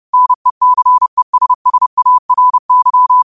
Morse code, heard as a keyed tone, about 30 wpm, 1000 Hz. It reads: NECESSARY